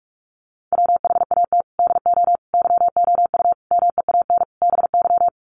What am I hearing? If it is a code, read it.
Morse code, heard as a keyed tone, about 44 wpm, 700 Hz. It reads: WHAT DO YOU MEAN BY
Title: Sets of 5 Words 45wpm